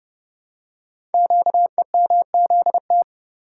Morse code, heard as a keyed tone, about 30 wpm, 700 Hz. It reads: QEMZT
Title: Code Groups - 5 Letters - 30wpm